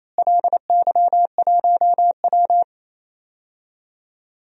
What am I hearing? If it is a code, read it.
Morse code, heard as a keyed tone, about 28 wpm, 700 Hz. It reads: LY1W
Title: DXCC-100 Call Signs - 4 Characters - Repeated 3x - 28wpm